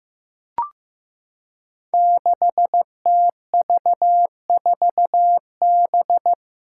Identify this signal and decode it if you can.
Morse code, heard as a keyed tone, about 15 wpm, 700 Hz. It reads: 6TV4B
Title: Five Challenging Letter-Number Combinations 15wpm